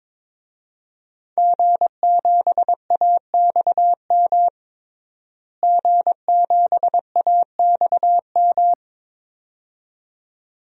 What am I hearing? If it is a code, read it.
Morse code, heard as a keyed tone, about 22 wpm, 700 Hz. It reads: G7AXM G7AXM
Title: DXCC-100 Call Signs - 5 Characters - Repeated 2x - 22wpm